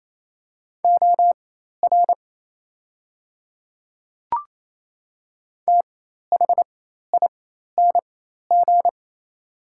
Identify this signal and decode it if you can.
Morse code, heard as a keyed tone, about 28 wpm, spaced as for 13 wpm, 700 Hz. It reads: OR THING
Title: 4x Character Spacing - Top 100 Words - 28wpm